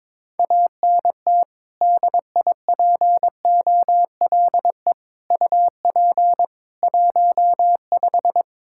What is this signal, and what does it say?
Morse code, heard as a keyed tone, about 22 wpm, 700 Hz. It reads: ANT DIPOLE UP 15
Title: Practice QSO - Exchange 2 - Info - 22wpm